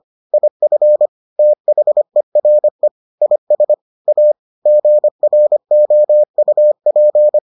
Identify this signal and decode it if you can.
Morse code, heard as a keyed tone, about 25 wpm, 600 Hz. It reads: IF THERE IS A GROUP